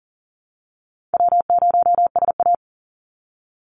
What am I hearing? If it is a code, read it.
Morse code, heard as a keyed tone, about 42 wpm, 700 Hz. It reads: W0SA
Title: Easy Call Signs - 40wpm